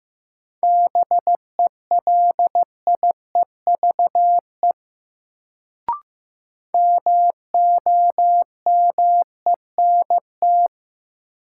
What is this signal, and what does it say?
Morse code, heard as a keyed tone, about 15 wpm, 700 Hz. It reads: BELIEVE MOMENT
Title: Sentences from Top 1000 Words - New Word Drill - 15wpm